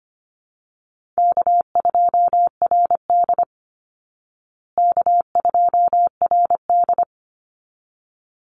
Morse code, heard as a keyed tone, about 25 wpm, 700 Hz. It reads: K2RD K2RD
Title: Call Signs Repeated 3x - 25wpm